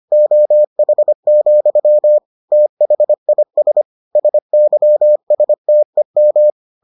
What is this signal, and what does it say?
Morse code, heard as a keyed tone, about 25 wpm, 600 Hz. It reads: OH, THIS SYSTEM